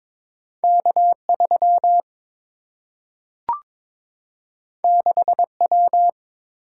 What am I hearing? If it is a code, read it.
Morse code, heard as a keyed tone, about 22 wpm, 700 Hz. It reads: K3 6W
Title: Single-Letter Single-Number 22wpm